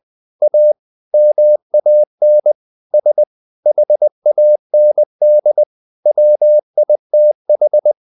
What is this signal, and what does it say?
Morse code, heard as a keyed tone, about 20 wpm, 600 Hz. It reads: A MAN S HAND WITH
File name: the old man and the sea part 2 20wpm